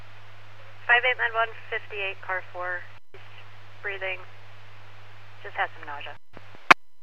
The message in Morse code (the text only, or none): none